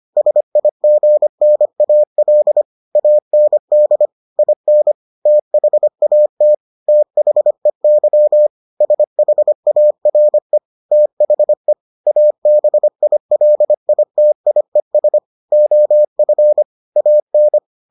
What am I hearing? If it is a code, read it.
Morse code, heard as a keyed tone, about 25 wpm, 600 Hz. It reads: SIGNAL AND IN THAT THEY SHARE THE ABILITIES OF AN